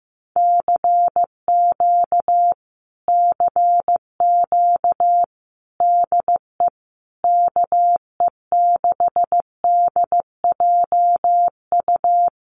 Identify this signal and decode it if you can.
Morse code, heard as a keyed tone, about 15 wpm, 700 Hz. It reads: CQ CQ DE KE6DJU
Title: Common QSO Phrases Encore 15wpm